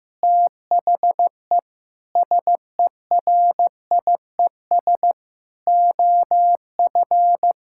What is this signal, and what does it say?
Morse code, heard as a keyed tone, about 15 wpm, 700 Hz. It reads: THE SERIES OF
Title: Sentences from Top 1000 Words - Review Part 5 - 15wpm